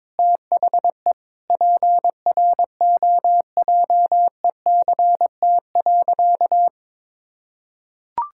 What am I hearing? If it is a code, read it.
Morse code, heard as a keyed tone, about 22 wpm, 700 Hz. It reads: THE PROJECT.
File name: practice / 22wpm / Sentences with 4-Word Phrases - 22wpm